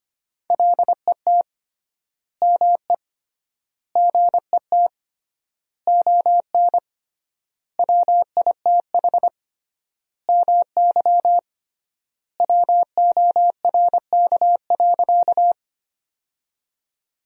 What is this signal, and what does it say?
Morse code, heard as a keyed tone, about 25 wpm, 700 Hz. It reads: LET ME GET ON WITH MY WORK.
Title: 3x Word-Spacing - Sentences from Top 300 Words - 25wpm